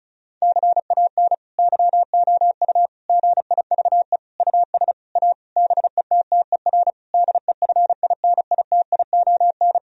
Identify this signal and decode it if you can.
Morse code, heard as a keyed tone, about 36 wpm, 700 Hz. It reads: CAN YOU GIVE US A BETTER DEFINITION
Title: Sentences from Top 1000 Words - Review Part 4 - 35wpm